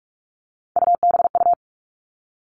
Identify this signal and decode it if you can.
Morse code, heard as a keyed tone, about 46 wpm, 700 Hz. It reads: UBU
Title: Three Challenging Letter-Number Combinations 45wpm